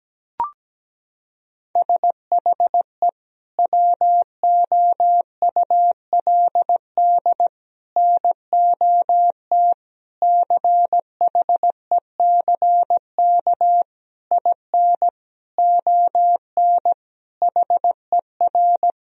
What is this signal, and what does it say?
Morse code, heard as a keyed tone, about 17 wpm, 700 Hz. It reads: SHE WOULD NOT CHECK IN ON HER
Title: Sentences from Top 1000 Words - Review Part 3 - 17wpm